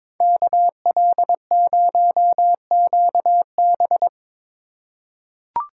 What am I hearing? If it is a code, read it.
Morse code, heard as a keyed tone, about 22 wpm, 700 Hz. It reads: KL0QB
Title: USA Call Signs - 2x Character Spacing - 22wpm